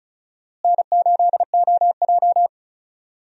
Morse code, heard as a keyed tone, about 36 wpm, 700 Hz. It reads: N8OJ
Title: USA Call Signs - 4 Characters - Repeated 2x - 35wpm